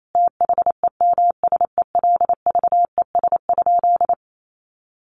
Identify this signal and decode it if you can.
Morse code, heard as a keyed tone, about 28 wpm, 700 Hz. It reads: THEMSELVES?
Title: Sentences from Top 800 Words 28wpm